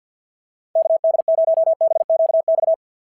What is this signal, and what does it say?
Morse code, heard as a keyed tone, about 49 wpm, 650 Hz. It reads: KD0BQX
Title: SOTA QSO Practice 50wpm